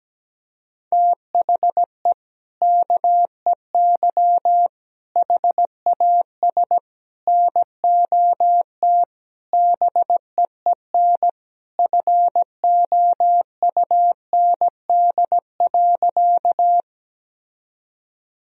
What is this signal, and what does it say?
Morse code, heard as a keyed tone, about 17 wpm, 700 Hz. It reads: THE KEY HAS NOT BEEN FOUND.
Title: Sentences from Top 1000 Words - Review Part 4 - 17wpm